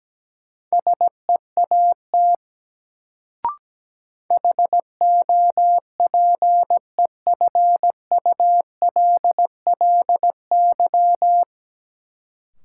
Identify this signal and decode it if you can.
Morse code, heard as a keyed tone, about 17 wpm, 700 Hz. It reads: SEAT HOPEFULLY
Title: Sentences from Top 1500 Words - New Word Drill - 17wpm